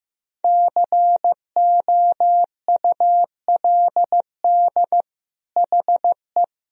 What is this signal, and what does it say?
Morse code, heard as a keyed tone, about 16 wpm, 700 Hz. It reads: COULD HE
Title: Sentences from Top 1000 Words - Review Part 2 - 15wpm